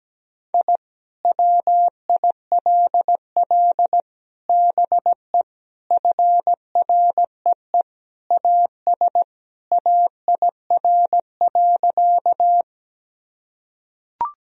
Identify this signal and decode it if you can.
Morse code, heard as a keyed tone, about 17 wpm, 700 Hz. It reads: I WILL BE FREE AS AIR.
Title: Sentences from Top 1000 Words - Review Part 1 - 17wpm